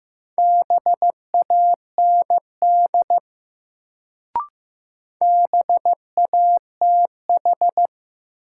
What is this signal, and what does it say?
Morse code, heard as a keyed tone, about 15 wpm, 700 Hz. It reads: BAND BATH
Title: Sentences from Top 1600 Words - New Word Drill - 15wpm